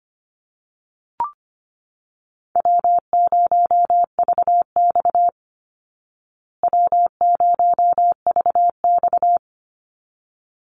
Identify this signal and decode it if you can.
Morse code, heard as a keyed tone, about 25 wpm, 700 Hz. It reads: W0VX W0VX
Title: Call Signs Repeated 3x - 25wpm